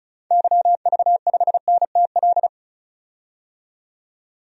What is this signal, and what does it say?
Morse code, heard as a keyed tone, about 36 wpm, 700 Hz. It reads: YV5NTL
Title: DXCC-100 Call Signs - 6 Characters - 35wpm